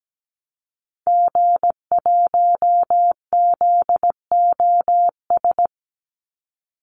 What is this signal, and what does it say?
Morse code, heard as a keyed tone, about 17 wpm, 700 Hz. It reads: G1ZOS
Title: Call Signs Repeated 2x - 17wpm